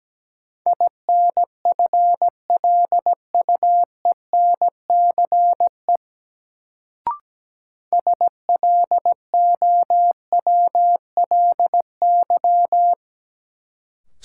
Morse code, heard as a keyed tone, about 17 wpm, 700 Hz. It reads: INFLUENCE SLOWLY
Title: Sentences from Top 1100 Words - New Word Drill Alt - 17wpm